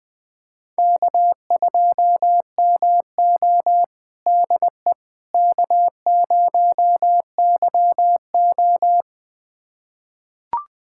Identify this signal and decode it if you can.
Morse code, heard as a keyed tone, about 20 wpm, 700 Hz. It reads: K2MO DE K0YO
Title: USA Callsign DE Callsign - 4 Characters - 20wpm